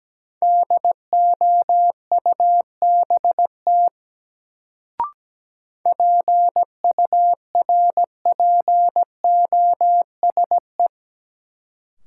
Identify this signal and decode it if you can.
Morse code, heard as a keyed tone, about 17 wpm, 700 Hz. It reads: DOUBT PURPOSE
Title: Sets of 5 Words - New Word Drill - 17wpm